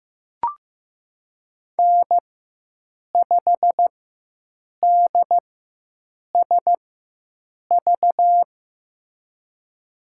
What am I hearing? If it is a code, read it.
Morse code, heard as a keyed tone, about 15 wpm, spaced as for 7 wpm, 700 Hz. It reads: N5DSV
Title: USA Call Signs - 5 Characters - 4x Character Spacing - 15wpm